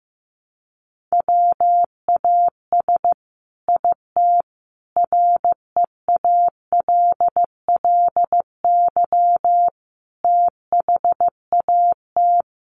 Morse code, heard as a keyed tone, about 14 wpm, 700 Hz. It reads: WAS IT REALLY THAT
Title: Sentences from Top 1300 Words 15wpm